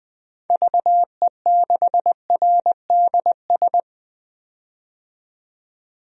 Morse code, heard as a keyed tone, about 20 wpm, 700 Hz. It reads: VE6RDS
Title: DXCC-100 Call Signs - 6 Characters - 20wpm